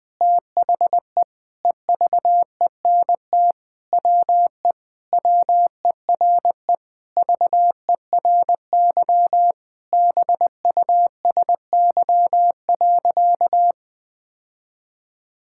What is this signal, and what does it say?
Morse code, heard as a keyed tone, about 21 wpm, 700 Hz. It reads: THE EVENT WE WERE VERY BUSY.
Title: Sentences with 7-Word Phrases - 20wpm